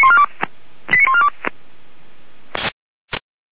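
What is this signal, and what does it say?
Five-tone selective call (ZVEI), heard as quick runs of five tones, tones of about 70 ms. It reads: 89142
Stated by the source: ZVEI five-tone selective call 89236 received by SDR.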